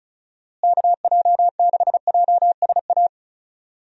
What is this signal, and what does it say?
Morse code, heard as a keyed tone, about 34 wpm, 700 Hz. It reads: KJ6JSA
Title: USA Call Signs - 6 Characters - Repeated 3x - 35wpm